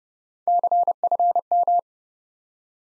Morse code, heard as a keyed tone, about 30 wpm, 700 Hz. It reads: CFM
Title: Top 90 Abbr 30WPM